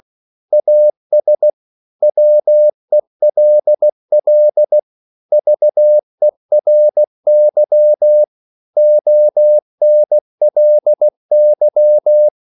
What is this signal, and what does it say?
Morse code, heard as a keyed tone, about 16 wpm, 600 Hz. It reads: AS WELL VERY ONLY
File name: CW / Practice / 500MostCommonEnglishWords16